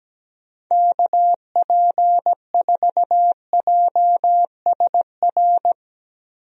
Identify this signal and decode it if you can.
Morse code, heard as a keyed tone, about 17 wpm, 700 Hz. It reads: KP4JSR
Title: USA Call Signs - 6 Characters - Repeated 2x - 17wpm